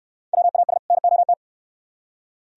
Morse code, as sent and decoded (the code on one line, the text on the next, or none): .-- .- ... / -. --- -
WAS NOT